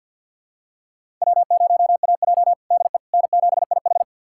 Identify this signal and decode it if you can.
Morse code, heard as a keyed tone, about 53 wpm, 700 Hz. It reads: W0AJ DE N7IH
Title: USA Callsign DE Callsign - 4 Characters - 50wpm